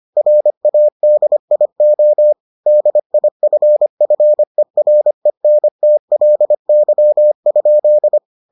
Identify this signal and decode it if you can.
Morse code, heard as a keyed tone, about 25 wpm, 600 Hz. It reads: RADIO DIFFERENTLY?